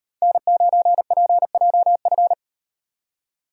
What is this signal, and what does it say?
Morse code, heard as a keyed tone, about 36 wpm, 700 Hz. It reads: N9PJF